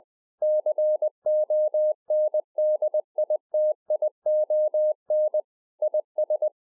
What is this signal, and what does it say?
Morse code, heard as a keyed tone, about 20 wpm, 610 Hz. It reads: CONDITION IS